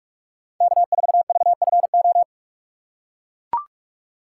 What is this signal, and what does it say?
Morse code, heard as a keyed tone, about 41 wpm, 700 Hz. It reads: K4VFO
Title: USA Call Signs - 5 Characters - 2x Character Spacing - 45wpm